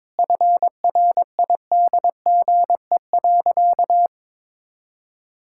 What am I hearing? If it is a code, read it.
Morse code, heard as a keyed tone, about 22 wpm, 700 Hz. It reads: FRIDGE.
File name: practice / 22wpm / Sentences with 4-Word Phrases - 22wpm